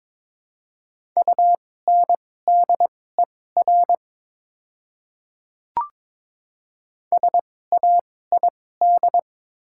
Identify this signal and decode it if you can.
Morse code, heard as a keyed tone, about 21 wpm, spaced as for 16 wpm, 700 Hz. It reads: UNDER SAID
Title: 2x Character Spacing - Top 300 Words - 22wpm